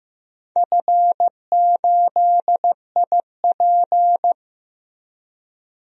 Morse code, heard as a keyed tone, about 15 wpm, 700 Hz. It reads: F8IP
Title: DXCC-100 Call Signs - 4 Characters - Repeated 2x - 15wpm